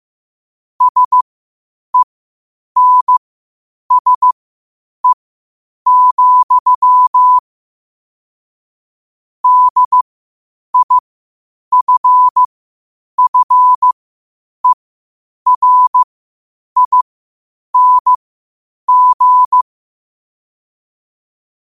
Morse code, heard as a keyed tone, about 15 wpm, spaced as for 8 wpm, 1000 Hz. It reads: SENSE, DIFFERING
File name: qotdfiles / 8wpm